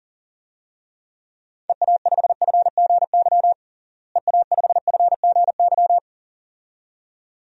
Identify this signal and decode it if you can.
Morse code, heard as a keyed tone, about 40 wpm, 700 Hz. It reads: EA5FGY EA5FGY
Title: DXCC-100 Call Signs - 6 Characters - Repeated 2x - 40wpm